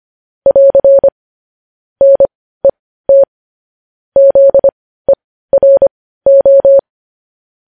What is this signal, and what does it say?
Morse code, heard as a keyed tone, about 25 wpm, spaced as for 15 wpm, 550 Hz. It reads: <AR> NET ZERO
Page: news-headlines-in-morse-code-at-15-wpm